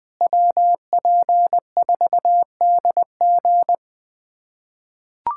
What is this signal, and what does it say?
Morse code, heard as a keyed tone, about 21 wpm, 700 Hz. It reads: WP4DG
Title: USA Callsign DE Callsign - 5 Characters - 20wpm